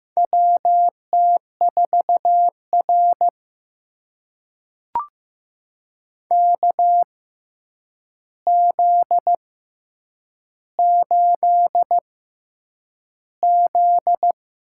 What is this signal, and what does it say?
Morse code, heard as a keyed tone, about 15 wpm, spaced as for 8 wpm, 700 Hz. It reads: WT4R KZ8Z
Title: USA Call Signs - 4 Characters - 6x Character Spacing - 15wpm